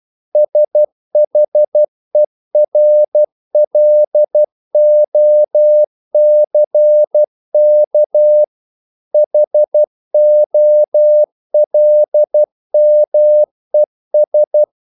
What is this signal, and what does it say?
Morse code, heard as a keyed tone, about 12 wpm, 600 Hz. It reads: SHERLOCK HOLMES
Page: English quotations in CW (telegraphy - morse)